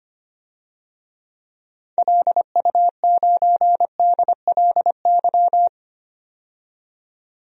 Morse code, heard as a keyed tone, about 25 wpm, 690 Hz. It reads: LU9DLY
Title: DXCC-100 Call Signs - 6 Characters - 25wpm